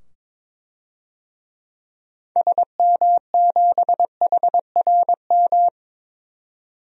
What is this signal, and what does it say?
Morse code, heard as a keyed tone, about 22 wpm, 700 Hz. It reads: SM7HRM
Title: DXCC-100 Call Signs - 6 Characters - 22wpm